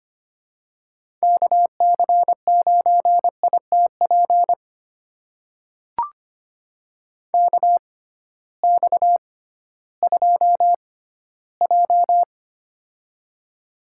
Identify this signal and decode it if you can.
Morse code, heard as a keyed tone, about 25 wpm, spaced as for 15 wpm, 700 Hz. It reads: KC9ITP KX2J
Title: USA Call Signs - 6x Character Spacing - 25wpm